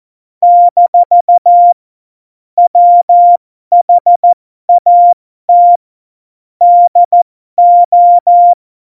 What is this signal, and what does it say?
Morse code, heard as a keyed tone, about 14 wpm, spaced as for 12 wpm, 700 Hz. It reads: - WHAT DO
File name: Morsekurs / Witze